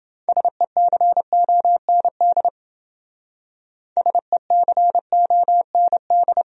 Morse code, heard as a keyed tone, about 28 wpm, 700 Hz. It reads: SECOND SECOND